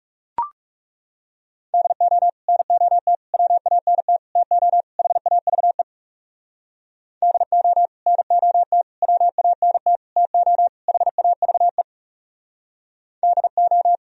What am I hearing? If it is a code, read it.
Morse code, heard as a keyed tone, about 37 wpm, 700 Hz. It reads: DO NOT WANT TO HAVE DO NOT WANT TO HAVE DO
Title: Speed-Racing - Sets of 5 Words 30wpm